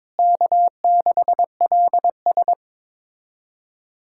Morse code, heard as a keyed tone, about 22 wpm, 700 Hz. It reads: K6LS
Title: USA Callsign DE Callsign - 4 Characters - 22wpm